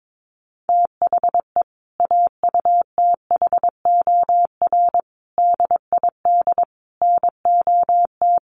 Morse code, heard as a keyed tone, about 22 wpm, 700 Hz. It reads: THE AUTHOR DID NOT
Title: Sentences from Top 1200 Words 22wpm